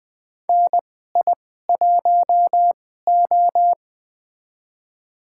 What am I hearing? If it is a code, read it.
Morse code, heard as a keyed tone, about 20 wpm, spaced as for 14 wpm, 700 Hz. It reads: NI1O
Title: USA Call Signs - 4 Characters - 2x Character Spacing - 20wpm